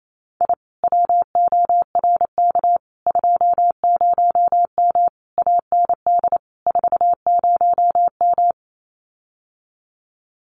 Morse code, heard as a keyed tone, about 28 wpm, 700 Hz. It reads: I WORK 20M AND 40M
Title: Common QSO Phrases Encore 28wpm